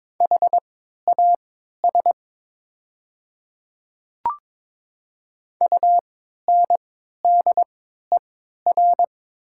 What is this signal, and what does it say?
Morse code, heard as a keyed tone, about 22 wpm, spaced as for 12 wpm, 690 Hz. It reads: HAS UNDER
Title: 3x Character Spacing - Top 200 Words - 22wpm